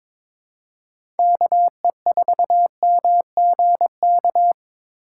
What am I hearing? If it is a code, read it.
Morse code, heard as a keyed tone, about 21 wpm, 700 Hz. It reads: KE4MGK
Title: USA Call Signs - 6 Characters - 8x Character Spacing - 22wpm